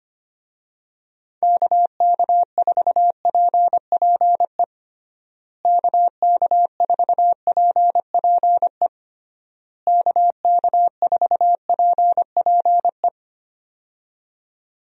Morse code, heard as a keyed tone, about 25 wpm, 700 Hz. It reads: KK4PPE KK4PPE KK4PPE
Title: USA Call Signs - 6 Characters - Repeated 3x - 25wpm